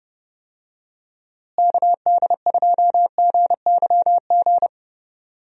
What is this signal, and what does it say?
Morse code, heard as a keyed tone, about 31 wpm, 700 Hz. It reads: KD2GYG
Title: USA Call Signs - 6 Characters - Repeated 2x - 30wpm